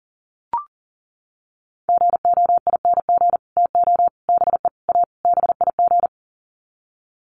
Morse code, heard as a keyed tone, about 40 wpm, 700 Hz. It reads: GOING TO BE A BIG
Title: Sets of 5 Words Encore2 40wpm